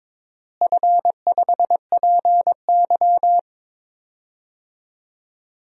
Morse code, heard as a keyed tone, about 22 wpm, 700 Hz. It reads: F5PY
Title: DXCC-100 Call Signs - 4 Characters - Repeated 2x - 22wpm